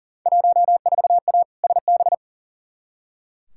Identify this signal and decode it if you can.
Morse code, heard as a keyed tone, about 40 wpm, 700 Hz. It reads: 14A SB
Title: Field Day Exchanges Encore - 40wpm